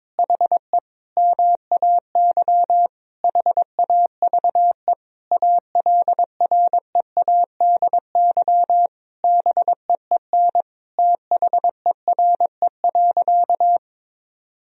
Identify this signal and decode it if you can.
Morse code, heard as a keyed tone, about 21 wpm, 700 Hz. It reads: HE MAY HAVE ALREADY BEEN THERE.
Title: Sentences from Top 1000 Words - Review Part 3 - 22wpm